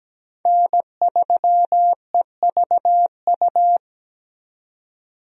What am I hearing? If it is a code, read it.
Morse code, heard as a keyed tone, about 17 wpm, 700 Hz. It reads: N3EVU
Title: USA Call Signs - 5 Characters - 17wpm